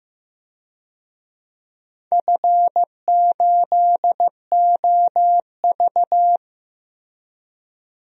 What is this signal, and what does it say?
Morse code, heard as a keyed tone, about 15 wpm, 700 Hz. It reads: F8OV
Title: DXCC-100 Call Signs - 4 Characters - Repeated 3x - 15wpm